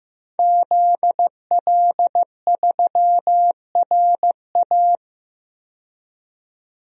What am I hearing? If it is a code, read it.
Morse code, heard as a keyed tone, about 15 wpm, 700 Hz. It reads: ZL3RA
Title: DXCC-100 Call Signs - 5 Characters - 15wpm